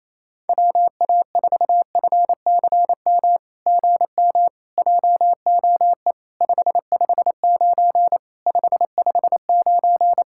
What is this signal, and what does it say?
Morse code, heard as a keyed tone, about 28 wpm, 700 Hz. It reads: WA4FCM GM JOE 559 559
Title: SOTA QSO Practice 28wpm